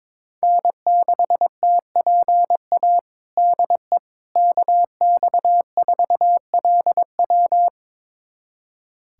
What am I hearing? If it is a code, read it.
Morse code, heard as a keyed tone, about 22 wpm, 700 Hz. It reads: N6TPA DE KX4LW
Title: USA Callsign DE Callsign - 5 Characters - 22wpm